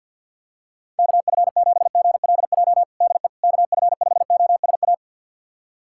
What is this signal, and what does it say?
Morse code, heard as a keyed tone, about 49 wpm, 700 Hz. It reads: KF7GLJ DE KF5OSA
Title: USA Callsign DE Callsign - 6 Characters - 50wpm